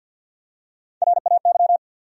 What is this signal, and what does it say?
Morse code, heard as a keyed tone, about 51 wpm, 700 Hz. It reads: RAY
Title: USA Common Names - Max 5 Letters - 50wpm